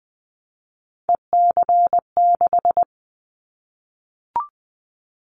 Morse code, heard as a keyed tone, about 20 wpm, 700 Hz. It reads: EC6
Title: Callsign Prefixes for Top 100 DXCC - 20wpm